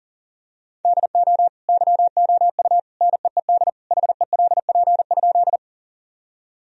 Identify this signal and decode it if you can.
Morse code, heard as a keyed tone, about 40 wpm, 700 Hz. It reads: DO YOU NEED HELP?
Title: Sentences from Top 1000 Words - Review Part 1 - 40wpm